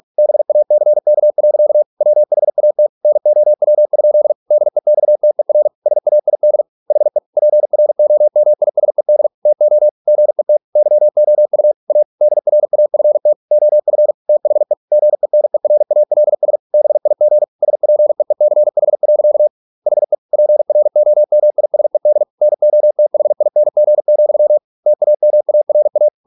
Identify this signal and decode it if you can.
Morse code, heard as a keyed tone, about 46 wpm, 600 Hz. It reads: BACK. WHAT NOW? DEXTER SAID HE PROMISED TO GET YOU A DRAFT OF THE GENERALS BIG SPEECH. HE PROMISED NOTHING, TAMARA